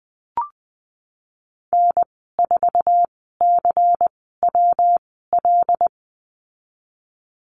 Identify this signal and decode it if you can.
Morse code, heard as a keyed tone, about 20 wpm, spaced as for 15 wpm, 700 Hz. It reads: N4CWL
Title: 154 Lesson 25g - Letter p - Call Signs 20wpm